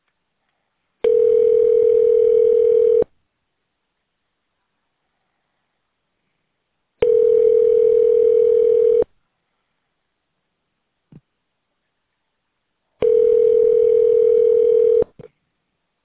Ringback tone (North America), 440 + 480 Hz.